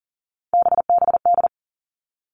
Morse code, heard as a keyed tone, about 40 wpm, 700 Hz. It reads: BBD
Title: All Cut Numbers - 3-Digit 40wpm